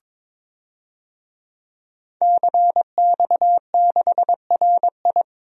Morse code, heard as a keyed tone, about 21 wpm, 700 Hz. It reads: CX6RI
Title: DXCC-100 Call Signs - 5 Characters - 22wpm